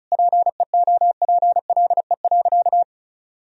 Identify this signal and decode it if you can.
Morse code, heard as a keyed tone, about 34 wpm, 700 Hz. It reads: PEOPLE.
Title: Sentences from Top 300 Words and Abbreviations 35wpm